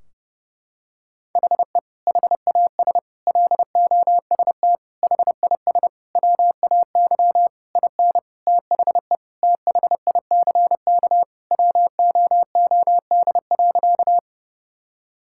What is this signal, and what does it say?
Morse code, heard as a keyed tone, about 31 wpm, 700 Hz. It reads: HE HAS LOST HIS WAY IN THE THICK WOOD.
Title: Sentences from Top 1600 Words 30wpm